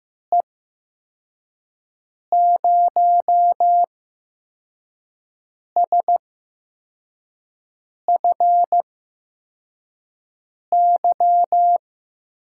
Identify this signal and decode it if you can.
Morse code, heard as a keyed tone, about 15 wpm, spaced as for 4 wpm, 700 Hz. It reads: E0SFY
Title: USA Call Signs - 6 Characters - 8x Character Spacing - 15wpm